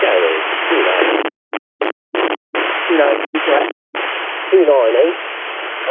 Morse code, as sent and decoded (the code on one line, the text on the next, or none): none